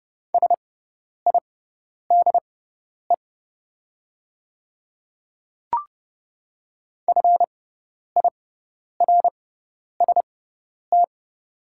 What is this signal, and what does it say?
Morse code, heard as a keyed tone, about 30 wpm, spaced as for 10 wpm, 700 Hz. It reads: SIDE FIRST
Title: 6x Character Spacing - Top 200 Words - 30wpm